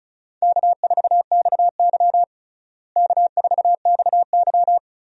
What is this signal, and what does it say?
Morse code, heard as a keyed tone, about 35 wpm, 700 Hz. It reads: K4XY K4XY
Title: USA Call Signs - 4 Characters - Repeated 3x - 35wpm